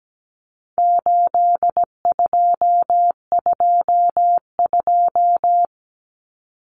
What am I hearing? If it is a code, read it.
Morse code, heard as a keyed tone, about 17 wpm, 700 Hz. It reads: 8222
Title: Common Cut Numbers - 4-Digit 17wpm